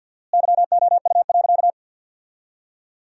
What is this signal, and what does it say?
Morse code, heard as a keyed tone, about 52 wpm, 700 Hz. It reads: YOU.
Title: 4x Word-Spacing - Sentences from Top 400 Words - 50wpm